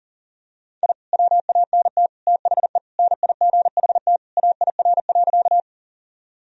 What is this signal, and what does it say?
Morse code, heard as a keyed tone, about 37 wpm, 700 Hz. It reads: I WANT THE NIGHT AIR.
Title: Sentences from Top 1000 Words - Review Part 1 - 40wpm